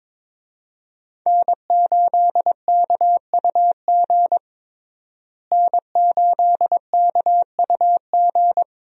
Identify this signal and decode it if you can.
Morse code, heard as a keyed tone, about 22 wpm, 700 Hz. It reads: N8KUG N8KUG
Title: USA Call Signs - 5 Characters - Repeated 3x - 22wpm